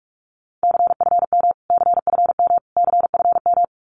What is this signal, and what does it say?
Morse code, heard as a keyed tone, about 44 wpm, 700 Hz. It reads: CFM CFM CFM
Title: Abbreviations Repeated 3x - 45wpm